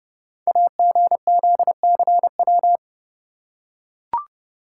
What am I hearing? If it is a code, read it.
Morse code, heard as a keyed tone, about 30 wpm, 700 Hz. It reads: AGZCW
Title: Code Groups - 5 Letters - 30wpm